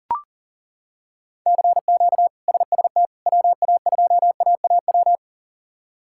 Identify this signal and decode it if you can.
Morse code, heard as a keyed tone, about 37 wpm, 700 Hz. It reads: CQ SST WA2AAW
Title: K1USN SST QSO Practice - 40wpm